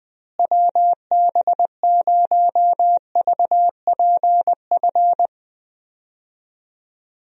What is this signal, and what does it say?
Morse code, heard as a keyed tone, about 20 wpm, 700 Hz. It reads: WB0VPF
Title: USA Call Signs - 6 Characters - Repeated 3x - 20wpm